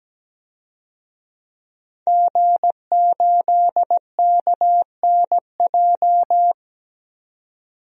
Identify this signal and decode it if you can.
Morse code, heard as a keyed tone, about 17 wpm, 700 Hz. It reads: G8KNJ
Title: DXCC-100 Call Signs - 5 Characters - 17wpm